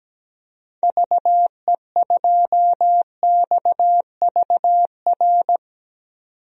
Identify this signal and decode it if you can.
Morse code, heard as a keyed tone, about 17 wpm, 700 Hz. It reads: VE2XVR
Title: DXCC-100 Call Signs - 6 Characters - 17wpm